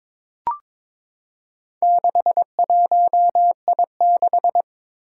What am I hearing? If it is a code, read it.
Morse code, heard as a keyed tone, about 23 wpm, 700 Hz. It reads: 61I6
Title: Four Challenging Letter-Number Combinations 22wpm